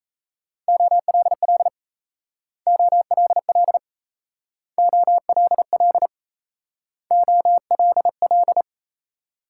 Morse code, heard as a keyed tone, about 34 wpm, 700 Hz. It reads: OLL OLL OLL OLL
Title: Speed-Racing Multi-letter Phonemes - 28wpm